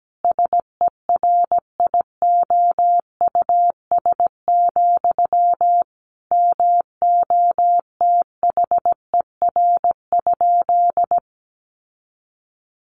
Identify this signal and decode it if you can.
Morse code, heard as a keyed tone, about 17 wpm, 700 Hz. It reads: SERIOUS, MOTHER?
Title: Sentences from Top 1100 Words 17wpm